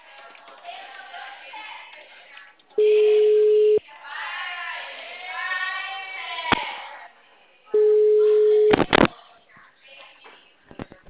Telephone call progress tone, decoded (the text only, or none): ringback tone